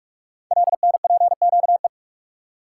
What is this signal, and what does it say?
Morse code, heard as a keyed tone, about 44 wpm, 700 Hz. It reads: LNPQE
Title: Code Groups - 5 Letters - 45wpm